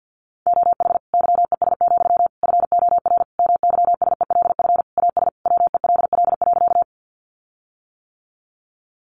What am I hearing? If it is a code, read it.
Morse code, heard as a keyed tone, about 50 wpm, 700 Hz. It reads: OH YES, FOR MYSELF AS WELL.
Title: Sentences from Top 1300 Words 50wpm